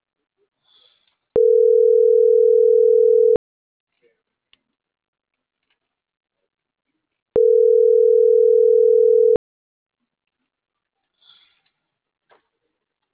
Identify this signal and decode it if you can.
Ringback tone (North America), 440 + 480 Hz.
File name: CENTRAL MAERSK / out polanco.msk - .36636